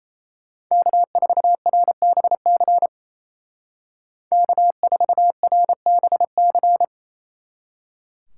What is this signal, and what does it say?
Morse code, heard as a keyed tone, about 33 wpm, 700 Hz. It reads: K4RBC K4RBC
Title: Speed Racing - USA Call Signs 28wpm